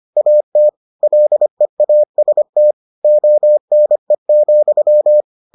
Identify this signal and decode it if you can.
Morse code, heard as a keyed tone, about 25 wpm, 600 Hz. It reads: AT LEAST ONE,